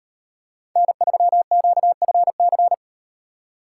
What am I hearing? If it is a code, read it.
Morse code, heard as a keyed tone, about 38 wpm, 700 Hz. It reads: N3QFC